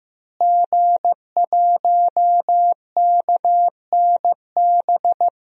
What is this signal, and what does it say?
Morse code, heard as a keyed tone, about 15 wpm, 700 Hz. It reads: G1KNB
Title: DXCC-100 Call Signs - 5 Characters - Repeated 2x - 15wpm